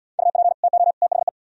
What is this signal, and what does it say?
Morse code, heard as a keyed tone, about 74 wpm, 700 Hz. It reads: UP TO THE